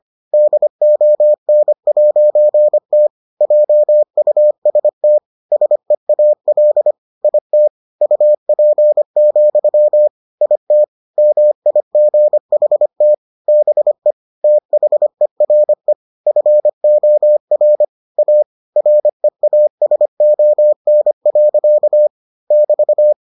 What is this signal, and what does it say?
Morse code, heard as a keyed tone, about 25 wpm, 600 Hz. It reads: DON'T JUST SEAL IT UP, IT MIGHT BE THERE FOR A REASON. <BT>